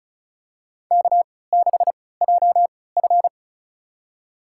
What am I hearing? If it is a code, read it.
Morse code, heard as a keyed tone, about 34 wpm, spaced as for 20 wpm, 700 Hz. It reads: K6JF
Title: USA Call Signs - 4 Characters - 3x Character Spacing - 35wpm